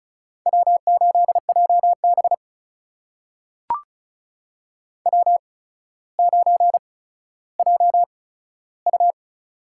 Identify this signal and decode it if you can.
Morse code, heard as a keyed tone, about 36 wpm, spaced as for 15 wpm, 700 Hz. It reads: W8JB W9JU
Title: USA Call Signs - 4 Characters - 8x Character Spacing - 35wpm